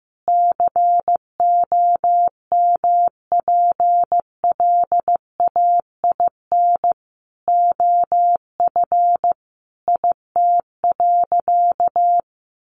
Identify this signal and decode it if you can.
Morse code, heard as a keyed tone, about 15 wpm, 700 Hz. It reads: COMPLAIN OF IT.
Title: Sentences from Top 1300 Words 15wpm